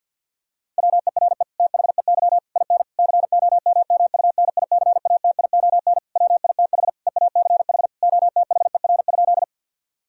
Morse code, heard as a keyed tone, about 50 wpm, 700 Hz. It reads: WERE THEY IN COMMUNICATION WITH EACH OTHER?
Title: Sentences from Top 1000 Words - Review Part 6 - 50wpm